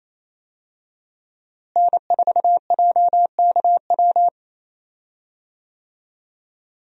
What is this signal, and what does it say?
Morse code, heard as a keyed tone, about 28 wpm, 700 Hz. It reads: N4JKW
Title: USA Call Signs - 5 Characters - Repeated 3x - 28wpm